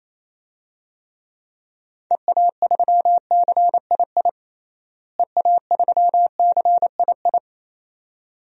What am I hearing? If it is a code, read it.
Morse code, heard as a keyed tone, about 28 wpm, 700 Hz. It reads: EA3CII EA3CII
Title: DXCC-100 Call Signs - 6 Characters - Repeated 2x - 28wpm